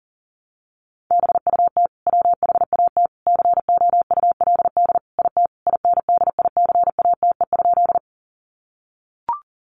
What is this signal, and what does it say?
Morse code, heard as a keyed tone, about 40 wpm, 700 Hz. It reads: BUT WHAT COULD IT INDICATE?
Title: Sentences from Top 1300 Words 40wpm